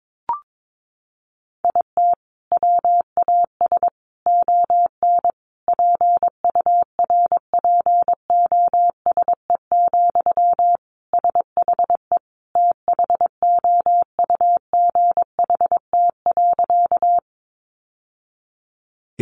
Morse code, heard as a keyed tone, about 22 wpm, 700 Hz. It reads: IT WAS ON PURPOSE, SHE THOUGHT.
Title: Sentences from Top 800 Words 22wpm